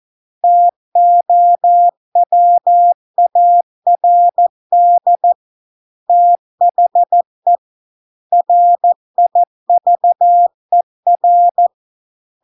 Morse code, heard as a keyed tone, about 14 wpm, 700 Hz. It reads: TOWARD THE RIVER